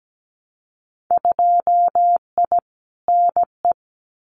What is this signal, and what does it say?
Morse code, heard as a keyed tone, about 17 wpm, 700 Hz. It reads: 2I NE
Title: Winter Field Day Exchanges Encore 17wpm